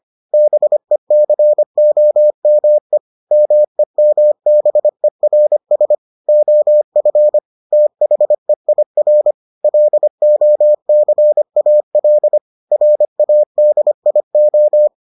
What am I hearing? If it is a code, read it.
Morse code, heard as a keyed tone, about 25 wpm, 600 Hz. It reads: BECOME MEMBERS OF THEIR LOCAL RADIO